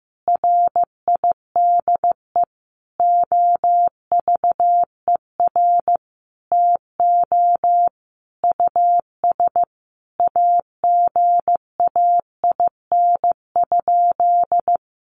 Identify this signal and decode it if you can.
Morse code, heard as a keyed tone, about 15 wpm, 700 Hz. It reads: RIDE OVER TO US AGAIN?
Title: Sentences from Top 1200 Words 15wpm